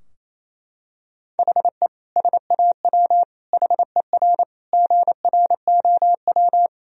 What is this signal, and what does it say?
Morse code, heard as a keyed tone, about 28 wpm, 700 Hz. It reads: HE SAW HER GROW
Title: Sentences from Top 1000 Words - Review Part 2 - 28wpm